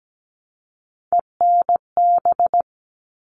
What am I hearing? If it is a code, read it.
Morse code, heard as a keyed tone, about 17 wpm, 700 Hz. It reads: ENB
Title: All Cut Numbers - 3-Digit 17wpm